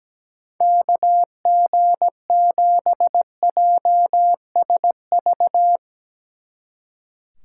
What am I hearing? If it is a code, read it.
Morse code, heard as a keyed tone, about 18 wpm, 700 Hz. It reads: KG7JSV
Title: Speed Racing - USA 2x3 Call Signs 17wpm